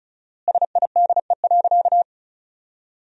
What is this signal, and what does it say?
Morse code, heard as a keyed tone, about 34 wpm, 700 Hz. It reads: SIDE.
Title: 4x Word-Spacing - Sentences from Top 400 Words - 35wpm